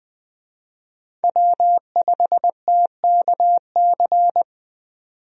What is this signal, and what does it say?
Morse code, heard as a keyed tone, about 21 wpm, 700 Hz. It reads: W5TKC
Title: USA Call Signs - 5 Characters - 20wpm